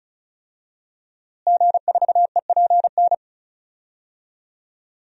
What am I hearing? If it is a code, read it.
Morse code, heard as a keyed tone, about 36 wpm, 700 Hz. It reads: G4EPN
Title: DXCC-100 Call Signs - 5 Characters - Repeated 2x - 35wpm